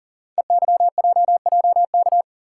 Morse code, heard as a keyed tone, about 42 wpm, 690 Hz. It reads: EYJJK
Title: Code Groups - 5 Letters - 40wpm